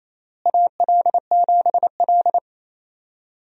Morse code, heard as a keyed tone, about 28 wpm, 700 Hz. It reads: AL7L
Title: USA Call Signs - 4 Characters - Repeated 3x - 28wpm